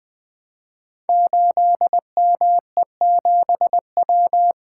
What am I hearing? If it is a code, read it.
Morse code, heard as a keyed tone, about 20 wpm, 700 Hz. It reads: 8ME7W
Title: Code Groups - 5 Letter:Numbers - 20wpm